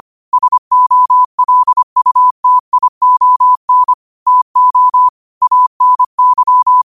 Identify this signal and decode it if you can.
Morse code, heard as a keyed tone, about 25 wpm, 1000 Hz. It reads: SOLUTION TO ANY
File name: qotdfiles / 25wpm